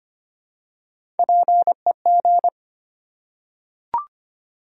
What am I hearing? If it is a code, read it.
Morse code, heard as a keyed tone, about 25 wpm, 700 Hz. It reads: PEG
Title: USA Common Names - Max 3 Letters - 25wpm